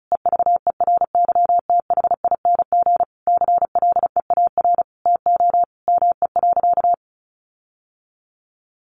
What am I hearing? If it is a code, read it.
Morse code, heard as a keyed tone, about 34 wpm, 700 Hz. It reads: EVERYTHING CLEAR TO ME.
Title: Sentences from Top 1300 Words 35wpm